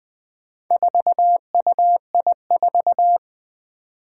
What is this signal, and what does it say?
Morse code, heard as a keyed tone, about 20 wpm, 700 Hz. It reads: 4UI4
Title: Four Challenging Letter-Number Combinations 20wpm